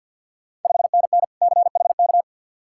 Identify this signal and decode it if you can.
Morse code, heard as a keyed tone, about 52 wpm, 690 Hz. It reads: 5NN CHK
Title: Ontario QSO Party Exchange 50wpm